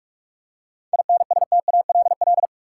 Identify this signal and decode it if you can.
Morse code, heard as a keyed tone, about 46 wpm, 700 Hz. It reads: INSTALL
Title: Seven Letter Words - 45wpm